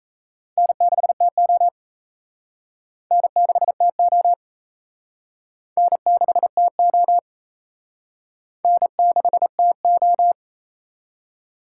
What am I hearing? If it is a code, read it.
Morse code, heard as a keyed tone, about 33 wpm, 700 Hz. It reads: N6TO N6TO N6TO N6TO
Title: Speed Racing - USA 1x2 Call Signs 28wpm